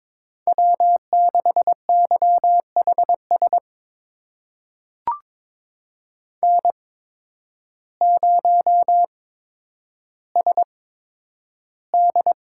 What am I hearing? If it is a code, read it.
Morse code, heard as a keyed tone, about 22 wpm, spaced as for 10 wpm, 700 Hz. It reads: W6YHS N0SD
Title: USA Call Signs - 5 Characters - 8x Character Spacing - 22wpm